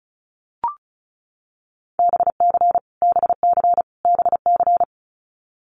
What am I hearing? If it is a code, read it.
Morse code, heard as a keyed tone, about 34 wpm, 700 Hz. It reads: BC BC BC
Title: Canadian Provinces and Territories - Repeated 3x - 35wpm